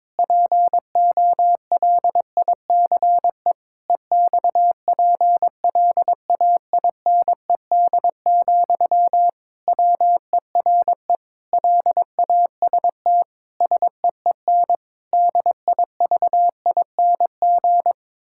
Morse code, heard as a keyed tone, about 22 wpm, 700 Hz. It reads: POLICE EXPLAINED, WERE LAST SEEN DIVING
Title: Pangram Challenge v2 - 22wpm